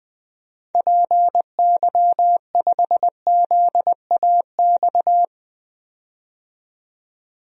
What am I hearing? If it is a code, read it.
Morse code, heard as a keyed tone, about 20 wpm, 700 Hz. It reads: PY5ZAX
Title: DXCC-100 Call Signs - 6 Characters - 20wpm